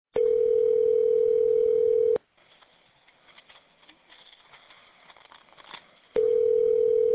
Ringback tone (North America), 440 + 480 Hz.